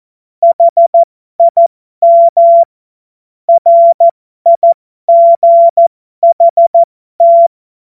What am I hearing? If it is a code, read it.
Morse code, heard as a keyed tone, about 14 wpm, spaced as for 12 wpm, 680 Hz. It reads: HIM RIGHT